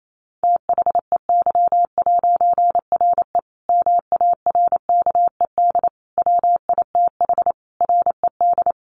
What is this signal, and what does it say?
Morse code, heard as a keyed tone, about 28 wpm, 700 Hz. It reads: THEY'RE MARKED WITH RED
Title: A Sound Of Thunder 28wpm